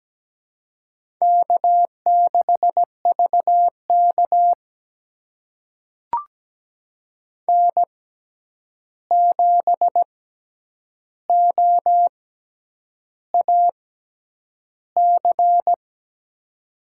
Morse code, heard as a keyed tone, about 17 wpm, spaced as for 8 wpm, 700 Hz. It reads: K6VK N7OAC
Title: USA Call Signs - 6x Character Spacing - 17wpm